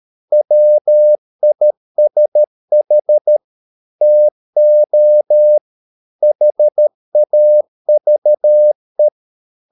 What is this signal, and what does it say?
Morse code, heard as a keyed tone, about 13 wpm, 600 Hz. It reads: WISH TO HAVE